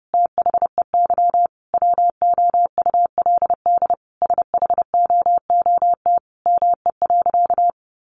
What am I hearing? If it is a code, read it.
Morse code, heard as a keyed tone, about 28 wpm, 700 Hz. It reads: THEY WOULD SHOOT ME.
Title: Sentences from Top 1500 Words 30wpm